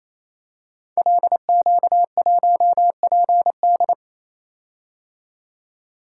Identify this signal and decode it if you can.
Morse code, heard as a keyed tone, about 27 wpm, 700 Hz. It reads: LQ1PD
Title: Code Groups - 5 Letter:Numbers - 28wpm